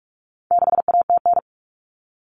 Morse code, heard as a keyed tone, about 44 wpm, 700 Hz. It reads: 6ATN
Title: All Cut Numbers - 4-Digit 45wpm